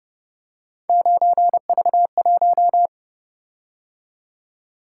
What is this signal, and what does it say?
Morse code, heard as a keyed tone, about 29 wpm, 700 Hz. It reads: 9V1